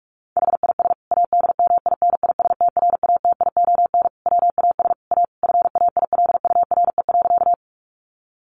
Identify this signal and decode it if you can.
Morse code, heard as a keyed tone, about 44 wpm, 700 Hz. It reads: HIS ADMINISTRATION WAS A FAILURE.
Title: Sentences from Top 1300 Words 45wpm